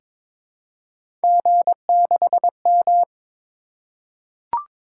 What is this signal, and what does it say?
Morse code, heard as a keyed tone, about 22 wpm, 700 Hz. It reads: G6M
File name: practice / 22wpm / Three Letter-Number Combinations - 22wpm